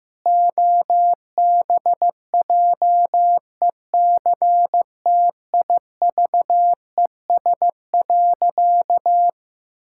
Morse code, heard as a keyed tone, about 15 wpm, 700 Hz. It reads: OBJECTIVES.
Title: Pangram Challenge v2 - 15wpm